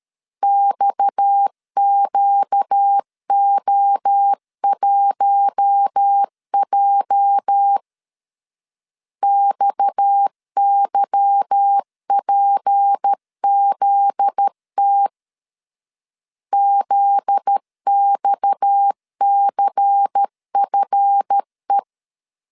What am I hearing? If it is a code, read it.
Morse code, heard as a keyed tone, about 13 wpm, 800 Hz. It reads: XQO1J XYPZT ZXCFE
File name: LEZIONI / 8 YJ45 / 8 YJ 12wpm 2235